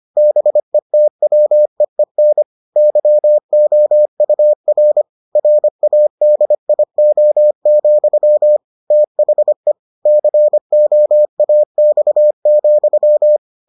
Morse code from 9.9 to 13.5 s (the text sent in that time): COAX,